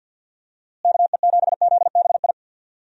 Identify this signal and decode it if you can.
Morse code, heard as a keyed tone, about 52 wpm, 700 Hz. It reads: KE7ZBI
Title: USA Call Signs - 6 Characters - Repeated 2x - 50wpm